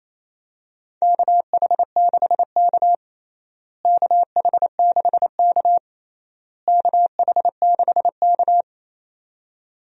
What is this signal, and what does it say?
Morse code, heard as a keyed tone, about 28 wpm, 700 Hz. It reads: KH6K KH6K KH6K
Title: DXCC-100 Call Signs - 4 Characters - Repeated 3x - 28wpm